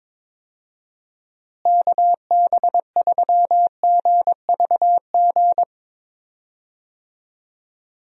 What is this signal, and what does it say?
Morse code, heard as a keyed tone, about 22 wpm, 700 Hz. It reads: KB3GVG
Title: USA Call Signs - 6 Characters - Repeated 3x - 22wpm